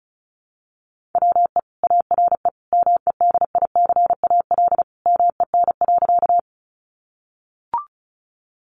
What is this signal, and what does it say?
Morse code, heard as a keyed tone, about 34 wpm, 700 Hz. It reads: WE ARE MEDICAL MEN.
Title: Sentences from Top 800 Words 35wpm